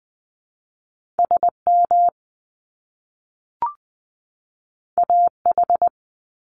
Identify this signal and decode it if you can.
Morse code, heard as a keyed tone, about 20 wpm, 700 Hz. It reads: SM AH
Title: Top Two Letter Combinations in Words 20wpm